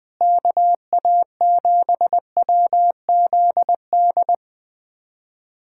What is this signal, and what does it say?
Morse code, heard as a keyed tone, about 21 wpm, 700 Hz. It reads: KA7WZD
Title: USA Callsign DE Callsign 20wpm